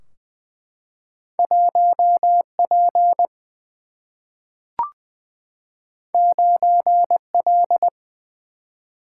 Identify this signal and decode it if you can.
Morse code, heard as a keyed tone, about 20 wpm, 700 Hz. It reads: 1P 9L
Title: Single-Letter Single-Number 20wpm